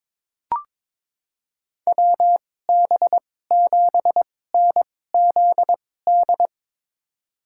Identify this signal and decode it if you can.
Morse code, heard as a keyed tone, about 22 wpm, spaced as for 16 wpm, 700 Hz. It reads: WB7NZD
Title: USA Call Signs - 6 Characters - 2x Character Spacing - 22wpm